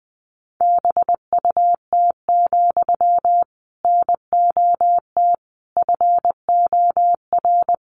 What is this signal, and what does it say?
Morse code, heard as a keyed tone, about 20 wpm, 700 Hz. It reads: BUT, NOT FOR
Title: Sentences from Top 600 Words 20wpm